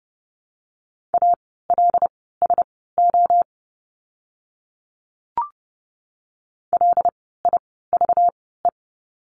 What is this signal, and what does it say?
Morse code, heard as a keyed tone, about 30 wpm, spaced as for 17 wpm, 700 Hz. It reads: ALSO LIVE
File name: course / 30:10wpm / 237 Lesson 38f - Letter q - Words 30wpm